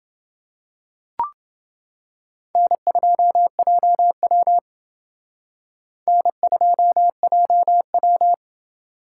Morse code, heard as a keyed tone, about 28 wpm, 700 Hz. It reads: N2JW N2JW